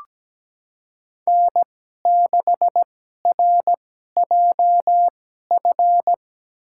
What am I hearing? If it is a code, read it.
Morse code, heard as a keyed tone, about 17 wpm, spaced as for 12 wpm, 700 Hz. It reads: N6RJF
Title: USA Call Signs - 5 Characters - 2x Character Spacing - 17wpm